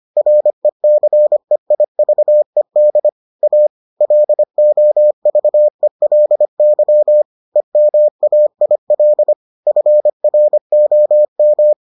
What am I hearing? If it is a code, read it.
Morse code, heard as a keyed tone, about 25 wpm, 600 Hz. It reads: RECEIVED A LOVELY EMAIL FROM